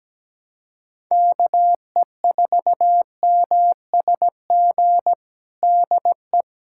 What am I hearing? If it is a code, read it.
Morse code, heard as a keyed tone, about 17 wpm, 700 Hz. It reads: KE4MSG DE
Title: USA Callsign DE Callsign - 6 Characters - 17wpm